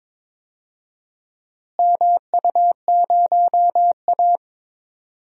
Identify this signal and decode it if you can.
Morse code, heard as a keyed tone, about 22 wpm, 700 Hz. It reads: MU0A
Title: DXCC-100 Call Signs - 4 Characters - Repeated 3x - 22wpm